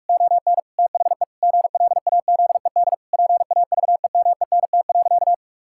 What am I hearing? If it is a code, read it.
Morse code, heard as a keyed tone, about 41 wpm, 700 Hz. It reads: ON THE GLAZED PAVEMENT.
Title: Pangram Challenge v2 - 45wpm